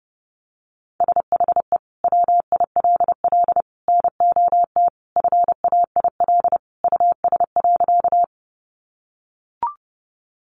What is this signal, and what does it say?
Morse code, heard as a keyed tone, about 30 wpm, 700 Hz. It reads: SHE WILL NOT FAIL US.
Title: Sentences from Top 1100 Words 30wpm